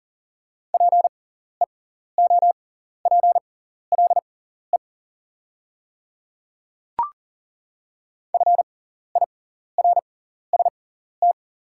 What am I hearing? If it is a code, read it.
Morse code, heard as a keyed tone, about 40 wpm, spaced as for 14 wpm, 690 Hz. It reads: PEOPLE FIRST
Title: 6x Character Spacing - Top 200 Words - 40wpm